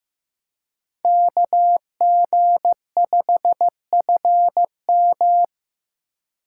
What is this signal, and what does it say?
Morse code, heard as a keyed tone, about 18 wpm, 700 Hz. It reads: KG5FM
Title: Speed Racing - USA 2x2 Call Signs 15wpm